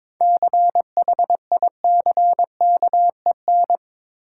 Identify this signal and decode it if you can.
Morse code, heard as a keyed tone, about 22 wpm, 690 Hz. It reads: CHICKEN
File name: Difficult Words with Hints - 22wpm